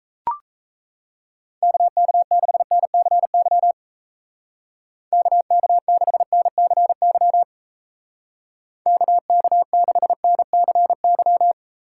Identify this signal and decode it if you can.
Morse code, heard as a keyed tone, about 37 wpm, 700 Hz. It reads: KK6NCY KK6NCY KK6NCY